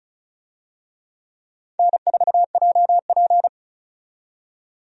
Morse code, heard as a keyed tone, about 36 wpm, 700 Hz. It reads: N4JP
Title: USA Call Signs - 4 Characters - 35wpm